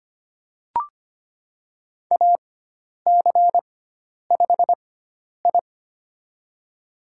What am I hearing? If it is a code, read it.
Morse code, heard as a keyed tone, about 25 wpm, spaced as for 10 wpm, 700 Hz. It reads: AC5I
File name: Lesson 16g - Letter c - Call Signs 25wpm